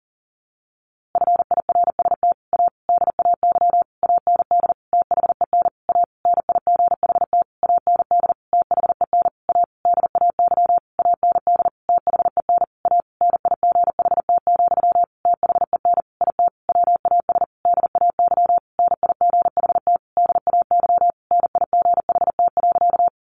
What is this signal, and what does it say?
Morse code, heard as a keyed tone, about 40 wpm, 700 Hz. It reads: FIRST A DAY AND THEN A NIGHT AND THEN A DAY AND THEN A NIGHT, THEN IT WAS DAY NIGHT DAY NIGHT.
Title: A Sound Of Thunder 40wpm